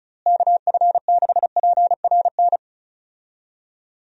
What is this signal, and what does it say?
Morse code, heard as a keyed tone, about 34 wpm, 700 Hz. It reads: KF6PRN
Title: USA Call Signs - 6 Characters - Repeated 2x - 35wpm